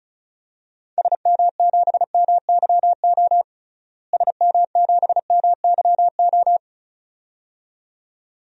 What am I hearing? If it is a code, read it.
Morse code, heard as a keyed tone, about 36 wpm, 700 Hz. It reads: SM7MYO SM7MYO
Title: DXCC-100 Call Signs - 6 Characters - Repeated 2x - 35wpm